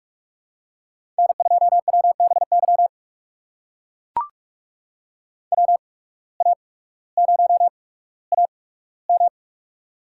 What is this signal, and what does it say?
Morse code, heard as a keyed tone, about 44 wpm, spaced as for 19 wpm, 700 Hz. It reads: N2WBY WA0AM
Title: USA Call Signs - 5 Characters - 8x Character Spacing - 45wpm